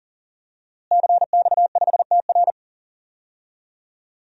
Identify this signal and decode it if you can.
Morse code, heard as a keyed tone, about 40 wpm, 700 Hz. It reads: CX5TR
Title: DXCC-100 Call Signs - 5 Characters - Repeated 3x - 40wpm